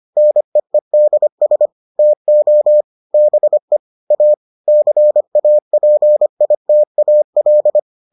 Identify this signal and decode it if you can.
Morse code, heard as a keyed tone, about 25 wpm, 600 Hz. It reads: NEEDS TO BE A CAPITAL